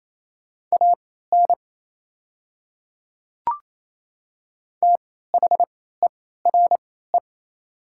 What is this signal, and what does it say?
Morse code, heard as a keyed tone, about 28 wpm, spaced as for 16 wpm, 700 Hz. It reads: AN THERE
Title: 3x Character Spacing - Top 100 Words - 28wpm